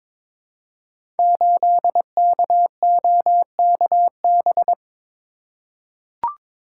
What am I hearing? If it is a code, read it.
Morse code, heard as a keyed tone, about 22 wpm, 700 Hz. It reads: 8KOKB
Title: Code Groups - 5 Letter:Numbers - 22wpm